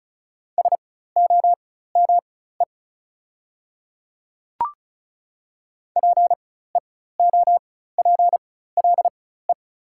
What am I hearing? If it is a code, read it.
Morse code, heard as a keyed tone, about 34 wpm, spaced as for 16 wpm, 700 Hz. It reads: SOME PEOPLE
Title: 4x Character Spacing - Top 200 Words - 35wpm